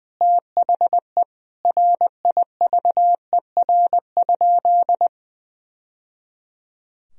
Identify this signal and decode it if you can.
Morse code, heard as a keyed tone, about 20 wpm, 700 Hz. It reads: THE RIVER?
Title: Sentences from Top 1000 Words - Review Part 4 - 20wpm